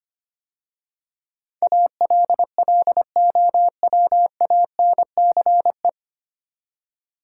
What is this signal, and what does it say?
Morse code, heard as a keyed tone, about 25 wpm, 700 Hz. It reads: ALLOWANCE
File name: practice / 25wpm / 9 Letter Words - First 3 Letters Given - 25wpm